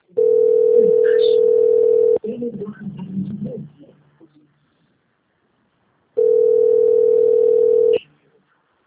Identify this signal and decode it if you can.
Ringback tone (North America), 440 + 480 Hz.